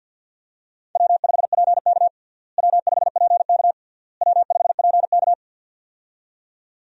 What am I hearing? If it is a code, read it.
Morse code, heard as a keyed tone, about 50 wpm, 700 Hz. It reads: W5PK W5PK W5PK
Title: USA Call Signs - 4 Characters - Repeated 3x - 50wpm